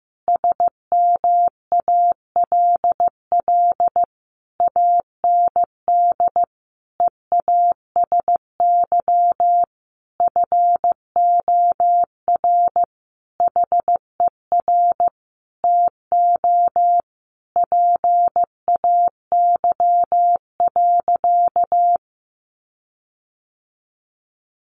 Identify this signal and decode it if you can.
Morse code, heard as a keyed tone, about 14 wpm, 700 Hz. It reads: SMALL AND EASY FOR HER TO PAY.
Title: Sentences from Top 1500 Words 15wpm